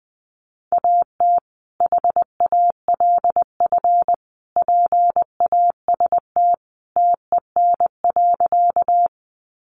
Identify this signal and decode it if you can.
Morse code, heard as a keyed tone, about 20 wpm, 700 Hz. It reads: AT HALF PAST TEN.
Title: Sentences from Top 600 Words 20wpm